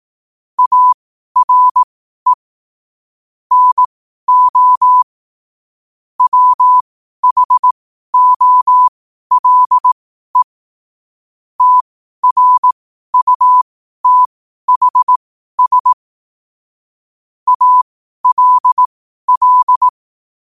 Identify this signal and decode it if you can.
Morse code, heard as a keyed tone, about 18 wpm, spaced as for 12 wpm, 1000 Hz. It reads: ARE NO WHOLE TRUTHS ALL
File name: qotdfiles / 12wpm